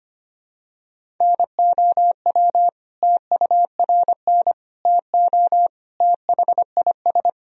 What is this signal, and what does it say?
Morse code, heard as a keyed tone, about 25 wpm, 700 Hz. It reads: NOW TURN TO THIS
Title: Sentences from Top 1000 Words - Review Part 1 - 25wpm